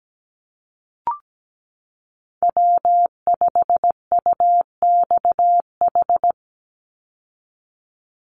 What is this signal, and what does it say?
Morse code, heard as a keyed tone, about 17 wpm, 700 Hz. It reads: W5UXH
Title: Call Signs Repeated 2x - 17wpm